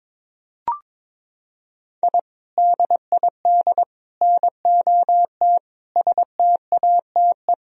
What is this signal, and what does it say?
Morse code, heard as a keyed tone, about 21 wpm, 700 Hz. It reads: I DID NOT STATE
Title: Sentences from Top 1000 Words - Review Part 2 - 22wpm